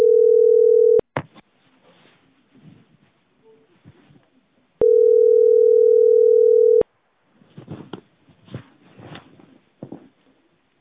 Ringback tone (North America), 440 + 480 Hz.